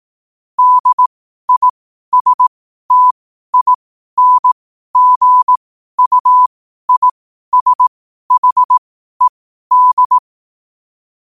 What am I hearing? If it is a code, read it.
Morse code, heard as a keyed tone, about 18 wpm, spaced as for 12 wpm, 1000 Hz. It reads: DISTINGUISHED
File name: qotdfiles / 12wpm